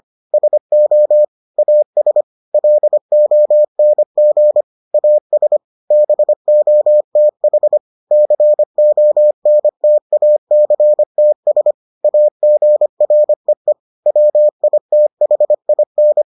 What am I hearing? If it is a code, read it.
Morse code, heard as a keyed tone, about 25 wpm, 600 Hz. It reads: SO AS LONG AS BOTH CONTACTS AGREE WITHIN